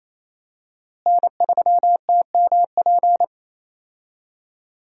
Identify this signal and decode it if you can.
Morse code, heard as a keyed tone, about 28 wpm, 700 Hz. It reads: N3TMP
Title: USA Call Signs - 5 Characters - Repeated 2x - 28wpm